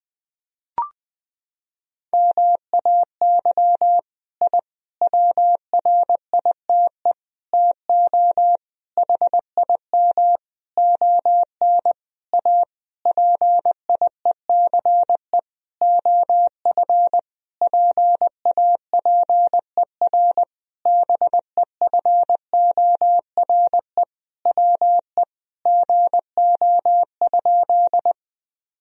Morse code, heard as a keyed tone, about 20 wpm, 700 Hz. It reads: MAY I WRITE TO HIM ON A PIECE OF PAPER BEFORE WE GO?
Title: Sentences from Top 1000 Words - Review Part 2 - 20wpm